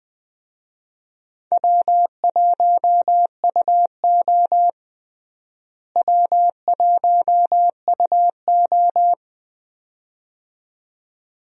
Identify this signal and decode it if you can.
Morse code, heard as a keyed tone, about 20 wpm, 700 Hz. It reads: W1UO W1UO
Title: USA Call Signs - 4 Characters - Repeated 2x - 20wpm